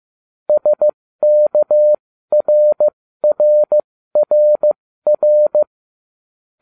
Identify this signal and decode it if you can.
Morse code, heard as a keyed tone, about 15 wpm, spaced as for 13 wpm, 600 Hz. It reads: SKRRRR